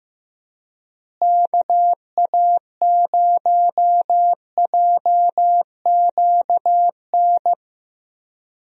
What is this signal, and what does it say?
Morse code, heard as a keyed tone, about 15 wpm, 700 Hz. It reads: KA0JQN
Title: USA Call Signs - 6 Characters - Repeated 3x - 15wpm